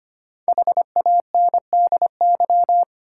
Morse code, heard as a keyed tone, about 25 wpm, 700 Hz. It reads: HANDY
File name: 5 Letter Words - First 3 Letters Given - 25wpm